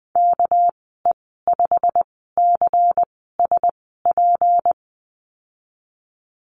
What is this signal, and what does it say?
Morse code, heard as a keyed tone, about 20 wpm, spaced as for 15 wpm, 700 Hz. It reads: KE5CSP
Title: 207 Lesson 33g - Letter j - Call Signs 20wpm